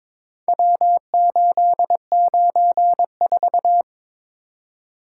Morse code, heard as a keyed tone, about 22 wpm, 700 Hz. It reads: W894
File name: practice / 22wpm / Four Letter-Number Combinations - 22wpm